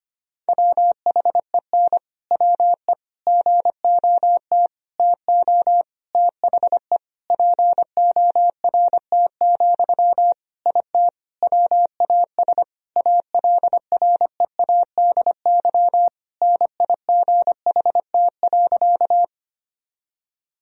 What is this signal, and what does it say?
Morse code, heard as a keyed tone, about 25 wpm, 700 Hz. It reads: WHEN WE GOT TO THE PORT, IT WAS ALREADY NIGHT.
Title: Sentences from Top 1000 Words - Review Part 1 - 25wpm